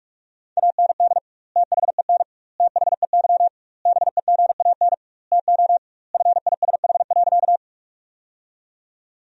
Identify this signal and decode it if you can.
Morse code, heard as a keyed tone, about 43 wpm, 700 Hz. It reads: AND THEN THEY BEGAN TO FISH.
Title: 2x Word-Spacing - Sentences from Top 300 Words - 45wpm